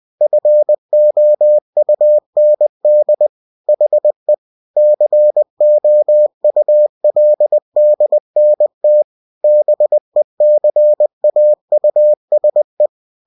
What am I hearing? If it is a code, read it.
Morse code, heard as a keyed tone, about 20 wpm, 600 Hz. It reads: FOUND HE COULDNT BECAUSE
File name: cw-books / winnie-the-pooh / winnie-20wpm-page-0019